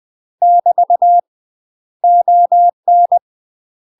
Morse code, heard as a keyed tone, about 20 wpm, 700 Hz. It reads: <BT> ON
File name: UK MASTER QSOs TRAINING / 20WPM / UK MASTER QSOs 0015